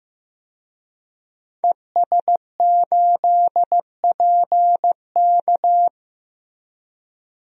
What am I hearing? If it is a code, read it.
Morse code, heard as a keyed tone, about 15 wpm, 700 Hz. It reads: ES8PK
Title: DXCC-100 Call Signs - 5 Characters - 15wpm